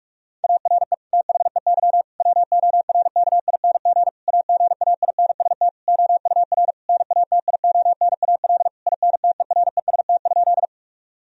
Morse code, heard as a keyed tone, about 41 wpm, 700 Hz. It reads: ARE THEY WORKING AGAINST OUR NATIONAL INTEREST?
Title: Sentences from Top 1000 Words - Review Part 4 - 45wpm